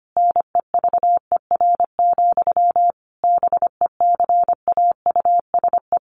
Morse code, heard as a keyed tone, about 25 wpm, 700 Hz. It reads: NEVER, BECAUSE
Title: Sentences from Top 1200 Words 25wpm